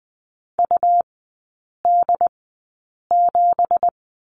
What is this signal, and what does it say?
Morse code, heard as a keyed tone, about 20 wpm, spaced as for 8 wpm, 700 Hz. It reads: UD7
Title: 198 Lesson 32e - Letter k - Three Characters 20wpm